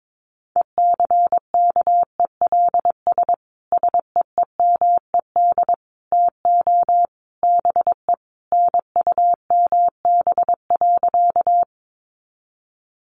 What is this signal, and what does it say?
Morse code, heard as a keyed tone, about 22 wpm, 700 Hz. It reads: ECKELS SEEMED TO BE NUMB.
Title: A Sound Of Thunder 22wpm